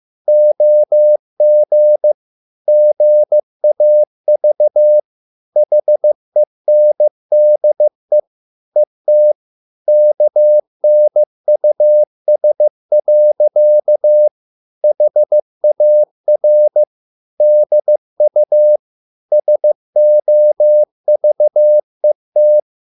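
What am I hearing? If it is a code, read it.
Morse code, heard as a keyed tone, about 15 wpm, 600 Hz. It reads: OG GAV HENDE ET KNUS. HAR DU SOVET
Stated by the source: Morse code at a slow 15 wpm from CW med Gnister.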